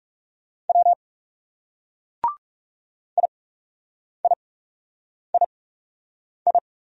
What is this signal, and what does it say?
Morse code, heard as a keyed tone, about 39 wpm, spaced as for 8 wpm, 700 Hz. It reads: W IIII